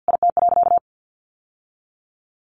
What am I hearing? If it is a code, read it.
Morse code, heard as a keyed tone, about 50 wpm, 700 Hz. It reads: IT.
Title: Sentences from Top 1400 Words 50wpm